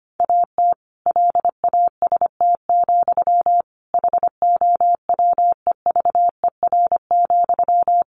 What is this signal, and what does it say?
Morse code, heard as a keyed tone, about 25 wpm, 700 Hz. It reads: AT LAST, HOWEVER,
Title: Sentences from Top 1400 Words 25wpm